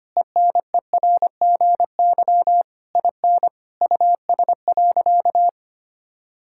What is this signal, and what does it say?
Morse code, heard as a keyed tone, about 25 wpm, 700 Hz. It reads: ENERGY IN US.
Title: Sentences from Top 1000 Words - Review Part 6 - 25wpm